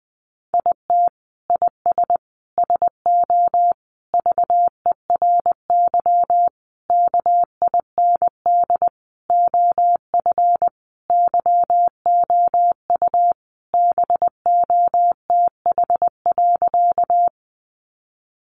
Morse code, heard as a keyed tone, about 20 wpm, 700 Hz. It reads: IT IS SO VERY KIND OF YOU BOTH.
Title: Sentences from Top 300 Words 20wpm